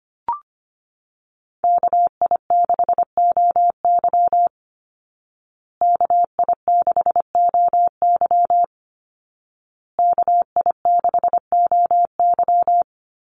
Morse code, heard as a keyed tone, about 25 wpm, 700 Hz. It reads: KI6OY KI6OY KI6OY
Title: Call Signs Repeated 2x - 25wpm